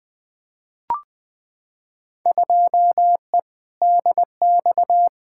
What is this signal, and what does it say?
Morse code, heard as a keyed tone, about 21 wpm, 700 Hz. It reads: 2E DX
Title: Field Day Exchanges - 20wpm